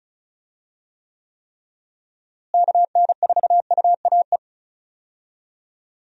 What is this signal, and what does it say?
Morse code, heard as a keyed tone, about 36 wpm, 700 Hz. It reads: KN4UAE
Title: USA Call Signs - 6 Characters - Repeated 3x - 35wpm